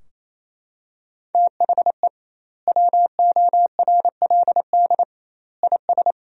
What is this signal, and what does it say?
Morse code, heard as a keyed tone, about 27 wpm, 700 Hz. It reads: THE WORLD IS
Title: 2x Word-Spacing - Sentences from Top 200 Words - 28wpm